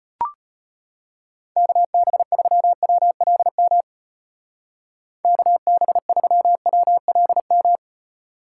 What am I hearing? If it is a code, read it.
Morse code, heard as a keyed tone, about 33 wpm, 700 Hz. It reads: KB3WLM KB3WLM